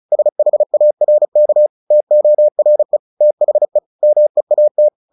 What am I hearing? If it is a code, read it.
Morse code, heard as a keyed tone, about 34 wpm, 600 Hz. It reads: SHARK TORE THE MEAT
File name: the old man and the sea 2 35wpm